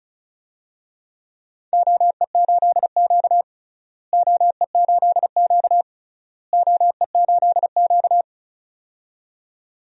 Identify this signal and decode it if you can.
Morse code, heard as a keyed tone, about 36 wpm, 690 Hz. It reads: OE8Q OE8Q OE8Q
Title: DXCC-100 Call Signs - 4 Characters - Repeated 3x - 35wpm